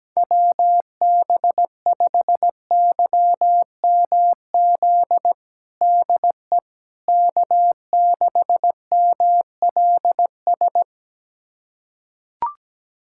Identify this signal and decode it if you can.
Morse code, heard as a keyed tone, about 17 wpm, 700 Hz. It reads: WB5YMZ DE K6MLS
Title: USA Callsign DE Callsign 17wpm